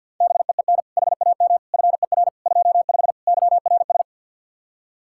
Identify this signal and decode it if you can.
Morse code, heard as a keyed tone, about 62 wpm, 700 Hz. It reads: BEEN HAM FER 25 YRS
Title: Speed-Racing Common QSO Phrases - 50wpm